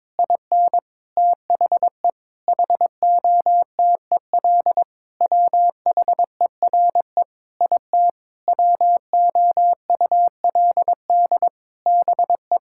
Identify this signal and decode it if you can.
Morse code, heard as a keyed tone, about 21 wpm, 700 Hz. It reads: IN THE HOTEL WHERE IT WOULD BE
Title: Sentences from Top 1600 Words 22wpm